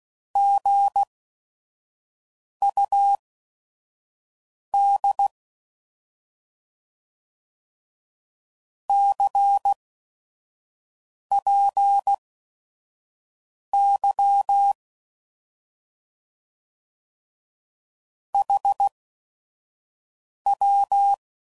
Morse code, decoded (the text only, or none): GUD CPY HW